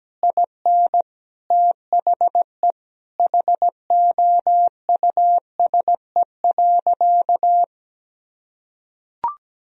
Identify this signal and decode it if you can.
Morse code, heard as a keyed tone, about 17 wpm, 700 Hz. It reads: IN THE HOUSE.
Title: Sentences from Top 500 Words and Abbreviations 17wpm